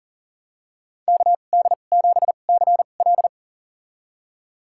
Morse code, heard as a keyed tone, about 37 wpm, spaced as for 28 wpm, 700 Hz. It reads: KD7CL
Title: USA Call Signs - 5 Characters - 2x Character Spacing - 40wpm